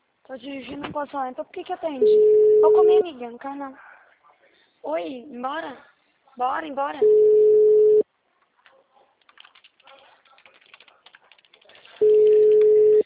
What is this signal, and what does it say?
Ringback tone (Europe, 425 Hz), with speech.